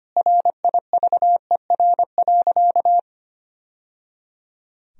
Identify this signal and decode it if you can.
Morse code, heard as a keyed tone, about 25 wpm, 700 Hz. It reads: RIVER.
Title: Sentences from Top 1000 Words - Review Part 2 - 25wpm